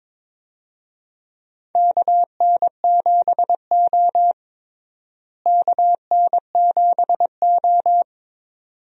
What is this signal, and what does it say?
Morse code, heard as a keyed tone, about 22 wpm, 700 Hz. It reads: KN7O KN7O
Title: USA Call Signs - 4 Characters - Repeated 2x - 22wpm